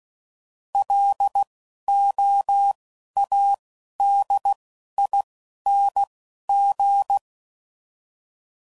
Morse code, heard as a keyed tone, about 16 wpm, spaced as for 11 wpm, 800 Hz. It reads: LOADING